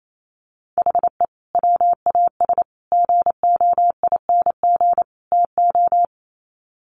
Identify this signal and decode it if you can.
Morse code, heard as a keyed tone, about 28 wpm, 700 Hz. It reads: HE WAS GOING TO
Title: Sets of 4 Words 28wpm